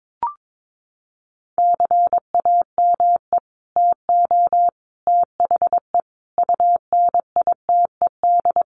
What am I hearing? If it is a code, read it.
Morse code, heard as a keyed tone, about 22 wpm, 700 Hz. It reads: CAME TO THE UNITED
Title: Sets of 5 Words Encore2 22wpm